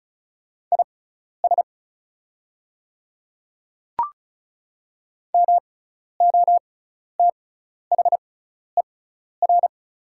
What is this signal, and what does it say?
Morse code, heard as a keyed tone, about 34 wpm, spaced as for 12 wpm, 690 Hz. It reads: IS MOTHER
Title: 6x Character Spacing - Top 300 Words - 35wpm